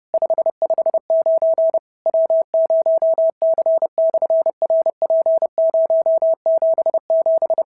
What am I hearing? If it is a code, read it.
Morse code, heard as a keyed tone, about 30 wpm, 650 Hz. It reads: 559 W0C/RP077
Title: SOTA QSO Practice 30wpm